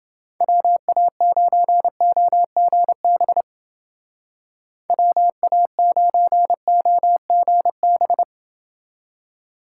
Morse code, heard as a keyed tone, about 29 wpm, 700 Hz. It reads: WA9OGB WA9OGB